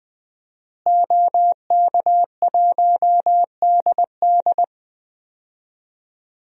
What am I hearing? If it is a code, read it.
Morse code, heard as a keyed tone, about 20 wpm, 700 Hz. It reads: OK1DD
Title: DXCC-100 Call Signs - 5 Characters - Repeated 2x - 20wpm